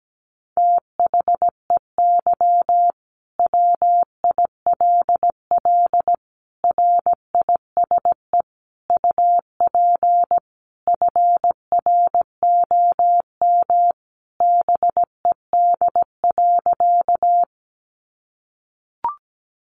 Morse code, heard as a keyed tone, about 17 wpm, 700 Hz. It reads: THEY WILL RISE UP FROM BED.
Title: Sentences from Top 900 Words 17wpm